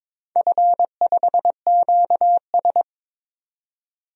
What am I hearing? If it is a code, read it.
Morse code, heard as a keyed tone, about 22 wpm, 690 Hz. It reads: F5QS
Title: DXCC-100 Call Signs - 4 Characters - 22wpm